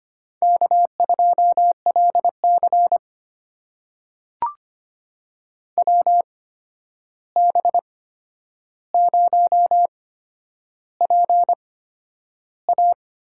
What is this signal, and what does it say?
Morse code, heard as a keyed tone, about 25 wpm, spaced as for 10 wpm, 700 Hz. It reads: K2LC WB0PA
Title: USA Call Signs - 8x Character Spacing - 25wpm